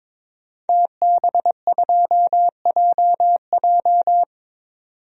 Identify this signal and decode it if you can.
Morse code, heard as a keyed tone, about 22 wpm, 700 Hz. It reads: TB2JJ
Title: DXCC-100 Call Signs - 5 Characters - 22wpm